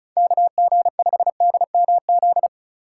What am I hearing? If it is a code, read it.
Morse code, heard as a keyed tone, about 36 wpm, 700 Hz. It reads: KG5DMZ
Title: USA Call Signs - 6 Characters - 2x Character Spacing - 35wpm